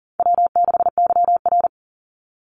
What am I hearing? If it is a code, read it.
Morse code, heard as a keyed tone, about 38 wpm, 700 Hz. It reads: W6YR
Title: Easy Call Signs Repeated 3x - 40wpm